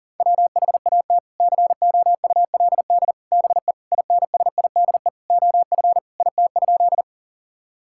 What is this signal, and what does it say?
Morse code, heard as a keyed tone, about 40 wpm, 700 Hz. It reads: WHAT COULD BE INSIDE OF IT?
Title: Sentences from Top 1000 Words - Review Part 6 - 40wpm